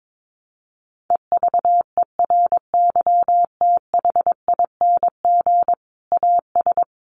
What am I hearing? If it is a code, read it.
Morse code, heard as a keyed tone, about 22 wpm, 700 Hz. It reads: EVERYTHING AS
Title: Sentences from Top 700 Words 22wpm